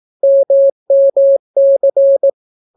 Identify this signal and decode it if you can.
Morse code, heard as a keyed tone, about 18 wpm, 550 Hz. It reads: MMC